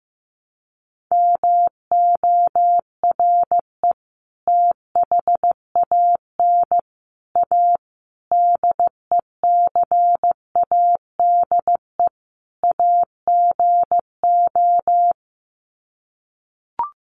Morse code, heard as a keyed tone, about 15 wpm, 700 Hz. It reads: MORE THAN A DECADE AGO
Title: Sets of 5 Words Encore2 15wpm